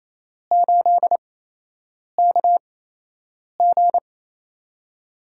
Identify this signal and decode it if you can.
Morse code, heard as a keyed tone, about 27 wpm, spaced as for 8 wpm, 700 Hz. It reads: 8KG
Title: USA Call Signs - 4 Characters - 8x Character Spacing - 28wpm